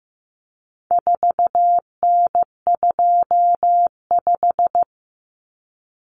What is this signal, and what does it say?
Morse code, heard as a keyed tone, about 15 wpm, 700 Hz. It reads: 4N25
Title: Common Cut Numbers - 4-Digit 15wpm